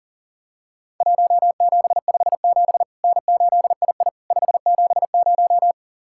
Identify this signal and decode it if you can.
Morse code, heard as a keyed tone, about 40 wpm, 700 Hz. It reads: 1757 N8II 570
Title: Worked All Europe DX Contest - QTC Exchange - 40wpm